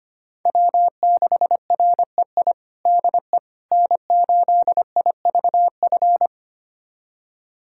Morse code, heard as a keyed tone, about 25 wpm, 700 Hz. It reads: W6REI DE N8IVF
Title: USA Callsign DE Callsign 25wpm